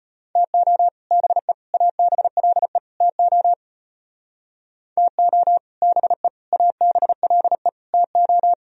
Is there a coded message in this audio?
Yes: Morse code from 0.4 to 8.5 s, heard as a keyed tone, about 32 wpm, 700 Hz. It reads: TO BE ABLE TO TO BE ABLE TO